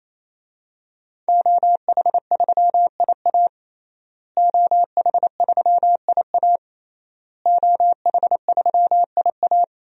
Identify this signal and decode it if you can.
Morse code, heard as a keyed tone, about 28 wpm, 700 Hz. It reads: OH3IA OH3IA OH3IA
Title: DXCC-100 Call Signs - 5 Characters - Repeated 3x - 28wpm